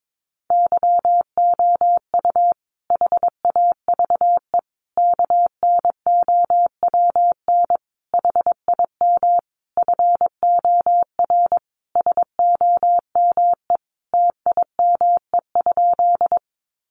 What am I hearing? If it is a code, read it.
Morse code, heard as a keyed tone, about 22 wpm, 700 Hz. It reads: YOU HAVE KNOWN HIM FOR SOME TIME?
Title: Sentences from Top 600 Words 22wpm